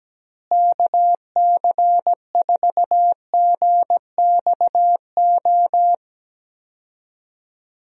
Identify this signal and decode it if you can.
Morse code, heard as a keyed tone, about 17 wpm, 700 Hz. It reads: KC4GXO
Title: USA Call Signs - 6 Characters - Repeated 3x - 17wpm